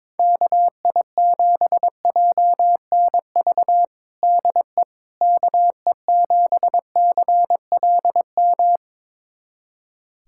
Morse code, heard as a keyed tone, about 22 wpm, 700 Hz. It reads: KI7JNV DE KE7CLM
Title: USA Callsign DE Callsign - 6 Characters - 22wpm